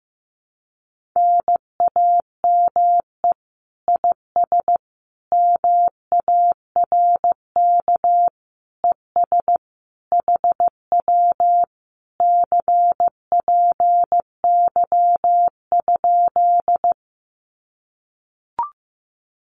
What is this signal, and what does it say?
Morse code, heard as a keyed tone, about 15 wpm, 700 Hz. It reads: NAME IS MARK ES HW CPY?
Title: Common QSO Phrases Encore 15wpm